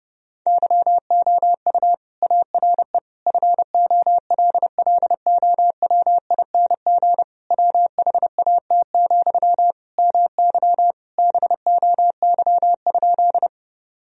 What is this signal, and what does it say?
Morse code, heard as a keyed tone, about 32 wpm, 700 Hz. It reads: YOU ARE FOLLOWING WHAT, MY BOY?
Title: Sentences from Top 1000 Words - Review Part 4 - 30wpm